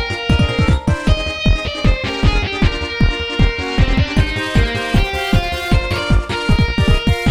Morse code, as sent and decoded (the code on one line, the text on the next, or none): none